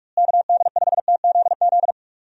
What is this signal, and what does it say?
Morse code, heard as a keyed tone, about 42 wpm, 700 Hz. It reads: KD5TZZ
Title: USA Call Signs - 6 Characters - Repeated 2x - 45wpm